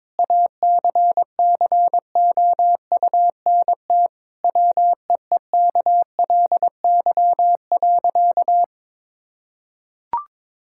Morse code, heard as a keyed tone, about 22 wpm, 700 Hz. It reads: ACCOUNT WEEKLY.
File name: Difficult Sentences with Hints - 22wpm